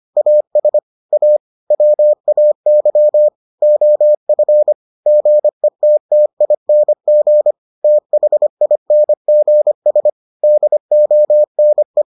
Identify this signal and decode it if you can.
Morse code, heard as a keyed tone, about 25 wpm, 600 Hz. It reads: AS A WAY OF GETTING THINGS DONE